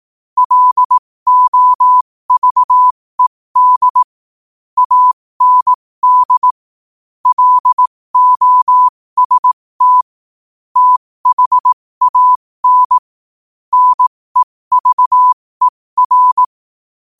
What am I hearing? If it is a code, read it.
Morse code, heard as a keyed tone, about 18 wpm, spaced as for 15 wpm, 1000 Hz. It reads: LOVED AND LOST THAN NEVER
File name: qotdfiles / 15wpm